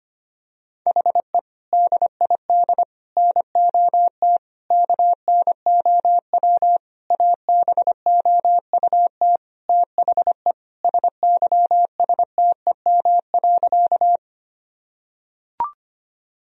Morse code, heard as a keyed tone, about 25 wpm, 700 Hz. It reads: HE DID NOT KNOW ABOUT THE SYSTEM.
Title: Sentences from Top 1000 Words - Review Part 3 - 25wpm